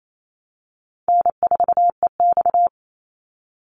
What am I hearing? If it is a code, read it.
Morse code, heard as a keyed tone, about 28 wpm, 700 Hz. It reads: N4EX
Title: NA SOTA Chaser Callsigns 28wpm